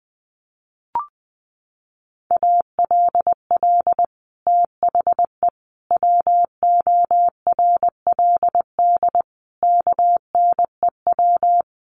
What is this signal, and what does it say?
Morse code, heard as a keyed tone, about 20 wpm, 700 Hz. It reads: ALL THE WORLD KNEW
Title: Sentences from Top 1200 Words 20wpm